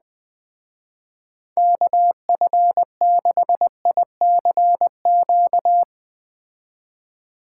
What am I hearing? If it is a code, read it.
Morse code, heard as a keyed tone, about 20 wpm, 700 Hz. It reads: KF6ICQ
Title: Speed Racing - USA Call Signs 17wpm